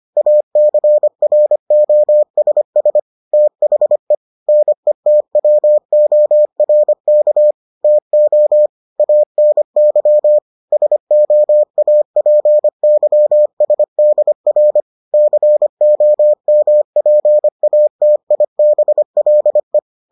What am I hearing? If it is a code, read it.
Morse code, heard as a keyed tone, about 25 wpm, 600 Hz. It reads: ACROSS THE NETWORK TO ANY SOAPYSDR COMPATIBLE